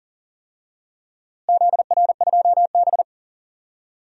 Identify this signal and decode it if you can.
Morse code, heard as a keyed tone, about 40 wpm, 700 Hz. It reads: ZR2B
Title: DXCC-100 Call Signs - 4 Characters - Repeated 2x - 40wpm